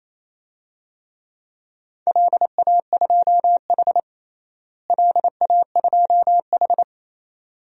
Morse code, heard as a keyed tone, about 27 wpm, 700 Hz. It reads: LA2H LA2H
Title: DXCC-100 Call Signs - 4 Characters - Repeated 2x - 28wpm